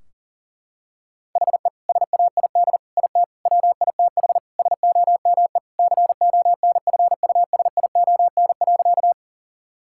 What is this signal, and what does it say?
Morse code, heard as a keyed tone, about 40 wpm, 700 Hz. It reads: HE SAID IT WITH SOME CONFUSION.
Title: Sentences from Top 1600 Words 40wpm